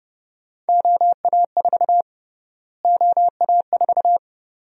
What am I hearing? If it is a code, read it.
Morse code, heard as a keyed tone, about 30 wpm, 700 Hz. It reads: OA4 OA4
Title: USA Call Sign Prefixes - Repeated 2x - 30wpm